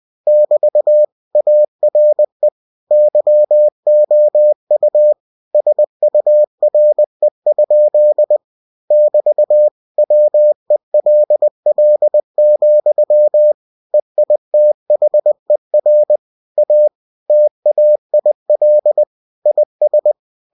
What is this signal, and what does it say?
Morse code, heard as a keyed tone, about 20 wpm, 600 Hz. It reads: <BT> ARE YOU SURE? <BT> WELL, EITHER A TAIL IS
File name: cw-books / winnie-the-pooh / winnie-20wpm-page-0027